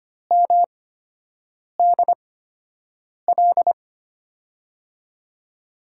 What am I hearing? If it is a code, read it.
Morse code, heard as a keyed tone, about 25 wpm, spaced as for 7 wpm, 700 Hz. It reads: MDL
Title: USA Call Signs - 8x Character Spacing - 25wpm